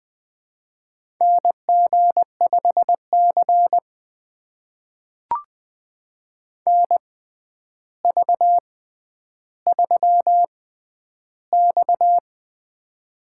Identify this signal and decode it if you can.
Morse code, heard as a keyed tone, about 20 wpm, spaced as for 10 wpm, 700 Hz. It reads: NG5C NV3X
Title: USA Call Signs - 4 Characters - 6x Character Spacing - 20wpm